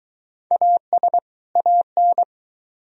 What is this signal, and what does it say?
Morse code, heard as a keyed tone, about 22 wpm, 700 Hz. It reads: AS AN